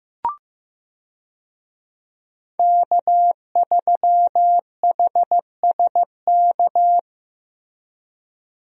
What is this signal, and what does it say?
Morse code, heard as a keyed tone, about 16 wpm, 700 Hz. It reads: K3HSK
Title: USA Call Signs - 5 Characters - Repeated 2x - 15wpm